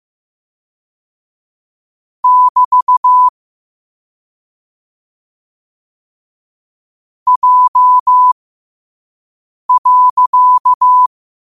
Morse code, heard as a keyed tone, about 15 wpm, spaced as for 5 wpm, 1000 Hz. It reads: <BT> J.
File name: qotdfiles / 5wpm